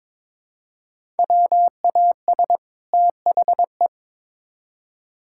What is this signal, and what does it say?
Morse code, heard as a keyed tone, about 22 wpm, 700 Hz. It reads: WAS THE
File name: Build A Phrase - 2-Word Phrases - 22wpm